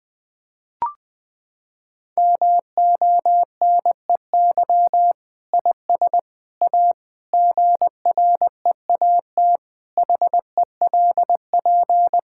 Morse code, heard as a keyed tone, about 20 wpm, 700 Hz. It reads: MONEY IS A GREAT HELP
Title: Sentences from Top 1600 Words 20wpm